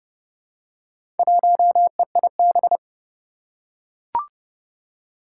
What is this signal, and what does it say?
Morse code, heard as a keyed tone, about 30 wpm, 700 Hz. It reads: 1EIB
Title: Four Challenging Letter-Number Combinations 30wpm